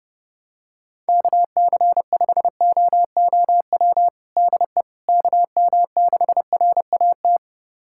Morse code, heard as a keyed tone, about 30 wpm, 700 Hz. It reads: KC5OOW DE KM6RAT
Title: USA Callsign DE Callsign - 6 Characters - 30wpm